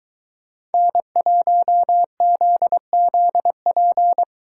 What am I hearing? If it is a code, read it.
Morse code, heard as a keyed tone, about 22 wpm, 700 Hz. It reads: N1ZZP